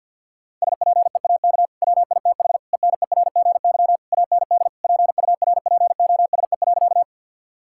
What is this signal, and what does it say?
Morse code, heard as a keyed tone, about 50 wpm, 700 Hz. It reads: SPEAK WITH ENERGY AND PURPOSE.
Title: Sentences from Top 1000 Words - Review Part 6 - 50wpm